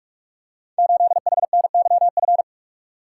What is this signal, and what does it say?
Morse code, heard as a keyed tone, about 44 wpm, 700 Hz. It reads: 8HNYF
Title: Code Groups - 5 Letter:Numbers - 45wpm